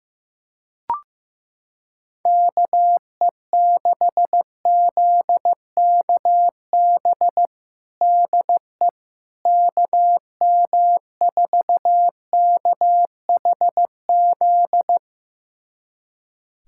Morse code, heard as a keyed tone, about 15 wpm, 700 Hz. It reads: KE6ZKB DE KM4KHZ
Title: USA Callsign DE Callsign 15wpm